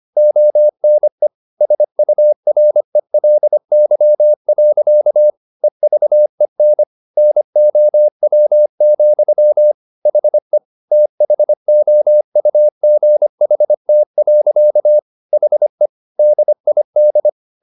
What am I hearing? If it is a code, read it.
Morse code, heard as a keyed tone, about 25 wpm, 600 Hz. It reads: ONE SURELY. EVEN NOW, HE THOUGHT. HE DID